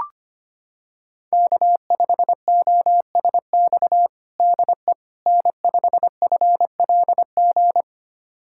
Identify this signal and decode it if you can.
Morse code, heard as a keyed tone, about 25 wpm, 700 Hz. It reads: K5OSX DE N5FLG
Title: USA Callsign DE Callsign - 5 Characters - 25wpm